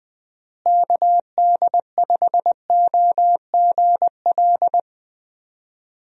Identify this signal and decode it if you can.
Morse code, heard as a keyed tone, about 19 wpm, 700 Hz. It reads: KD5OGL
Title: USA Call Signs - 6 Characters - Repeated 3x - 20wpm